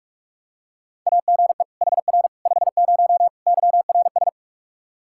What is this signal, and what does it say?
Morse code, heard as a keyed tone, about 41 wpm, 700 Hz. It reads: AGE HR 50 YRS
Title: K1USN SST QSO Practice 45wpm